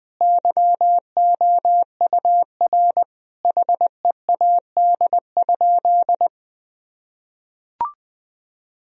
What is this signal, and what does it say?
Morse code, heard as a keyed tone, about 21 wpm, 700 Hz. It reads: YOUR HEAD?
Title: Sentences from Top 1000 Words - Review Part 3 - 20wpm